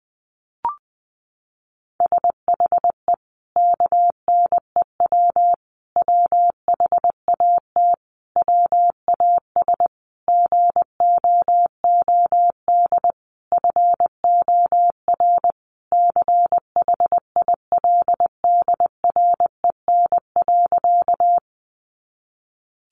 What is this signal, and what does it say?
Morse code, heard as a keyed tone, about 20 wpm, 700 Hz. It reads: SHE KNEW WHAT WAS GOOD FOR CHILDREN.
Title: Sentences from Top 400 Words 20wpm